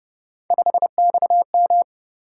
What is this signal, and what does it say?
Morse code, heard as a keyed tone, about 31 wpm, 700 Hz. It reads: 5XM
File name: Speed-Racing - Three Letter-Number Combinations - 25wpm